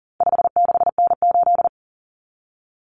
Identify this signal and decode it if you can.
Morse code, heard as a keyed tone, about 40 wpm, 700 Hz. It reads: 56N8
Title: Common Cut Numbers - 4-Digit 40wpm